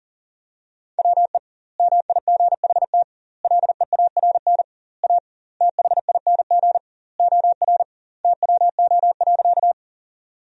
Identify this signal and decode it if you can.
Morse code, heard as a keyed tone, about 37 wpm, 700 Hz. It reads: WE MIGHT LEARN A THING OR TWO.
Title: 2x Word-Spacing - Sentences from Top 300 Words - 40wpm